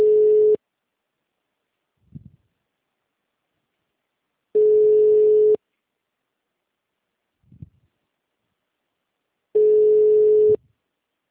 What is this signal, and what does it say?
Ringback tone (Europe, 425 Hz).
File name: out- -2011- - - .466